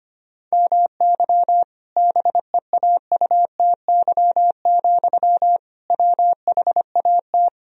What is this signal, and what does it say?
Morse code, heard as a keyed tone, about 25 wpm, 700 Hz. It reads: MY BEAUTY, WHAT
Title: Sentences from Top 1000 Words - Review Part 2 - 25wpm